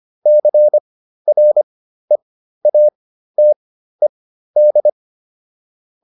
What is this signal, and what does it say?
Morse code, heard as a keyed tone, about 25 wpm, spaced as for 13 wpm, 600 Hz. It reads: CREATED